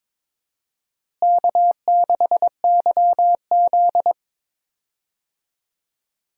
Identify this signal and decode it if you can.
Morse code, heard as a keyed tone, about 22 wpm, 700 Hz. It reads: K6YZ
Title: USA Call Signs - 4 Characters - Repeated 3x - 22wpm